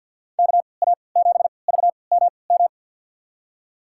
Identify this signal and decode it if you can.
Morse code, heard as a keyed tone, about 48 wpm, spaced as for 28 wpm, 700 Hz. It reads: KA7VMM
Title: USA Call Signs - 3x Character Spacing - 50wpm